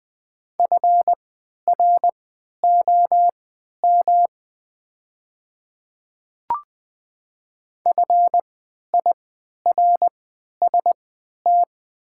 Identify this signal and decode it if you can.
Morse code, heard as a keyed tone, about 20 wpm, spaced as for 11 wpm, 700 Hz. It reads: FROM FIRST
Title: 3x Character Spacing - Top 100 Words - 20wpm